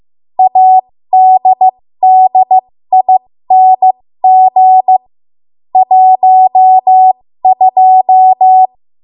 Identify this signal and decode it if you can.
Morse code, heard as a keyed tone, about 15 wpm, spaced as for 13 wpm, 750 Hz. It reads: ADDING 12
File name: Morse / 13WPM